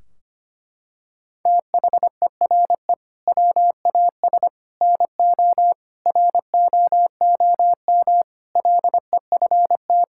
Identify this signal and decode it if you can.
Morse code, heard as a keyed tone, about 25 wpm, 700 Hz. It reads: THERE WAS NO ROOM LEFT
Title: Sentences from Top 1600 Words 25wpm